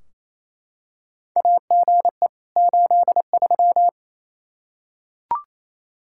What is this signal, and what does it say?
Morse code, heard as a keyed tone, about 28 wpm, 700 Hz. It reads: AGE 83
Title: Practice QSO - Exchange 3 - More Info - 28wpm